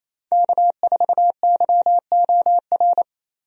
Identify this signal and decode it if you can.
Morse code, heard as a keyed tone, about 28 wpm, 700 Hz. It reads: K4YOR
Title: USA Call Signs - 5 Characters - Repeated 2x - 28wpm